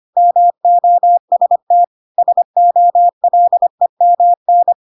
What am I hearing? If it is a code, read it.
Morse code, heard as a keyed tone, about 25 wpm, 700 Hz. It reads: MOST SOLEMN
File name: Pinocchio / Pinocchio ch1 25 25 ik1hge